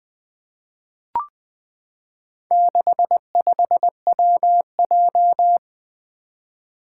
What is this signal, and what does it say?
Morse code, heard as a keyed tone, about 20 wpm, 700 Hz. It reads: 65WJ
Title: Four Challenging Letter-Number Combinations 20wpm